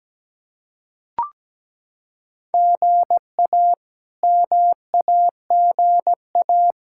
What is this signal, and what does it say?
Morse code, heard as a keyed tone, about 17 wpm, 700 Hz. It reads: GA MAGA
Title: K1USN SST QSO Practice - 17wpm